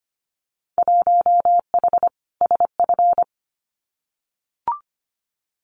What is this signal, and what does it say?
Morse code, heard as a keyed tone, about 25 wpm, 700 Hz. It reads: 1H SF
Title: Winter Field Day Exchanges 25wpm